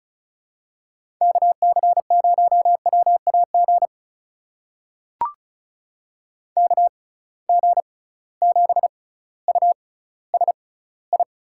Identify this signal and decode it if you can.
Morse code, heard as a keyed tone, about 36 wpm, spaced as for 18 wpm, 690 Hz. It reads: KC0WAG KG7USI
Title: USA Call Signs - 6 Characters - 6x Character Spacing - 35wpm